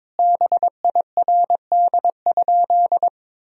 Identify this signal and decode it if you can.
Morse code, heard as a keyed tone, about 22 wpm, 700 Hz. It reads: BIRD?
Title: Sentences from Top 1000 Words - Review Part 2 - 22wpm